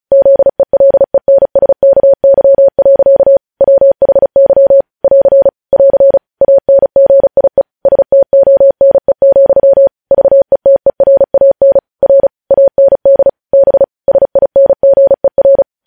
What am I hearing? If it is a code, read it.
Morse code, heard as a keyed tone, about 36 wpm, 550 Hz. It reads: ZELENSKY. WHY <AR> <AR> ANGIE STONE, VETERAN R AND B SINGER